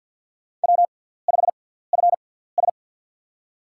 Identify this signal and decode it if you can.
Morse code, heard as a keyed tone, about 47 wpm, spaced as for 17 wpm, 700 Hz. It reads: W5FS
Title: USA Call Signs - 4 Characters - 6x Character Spacing - 50wpm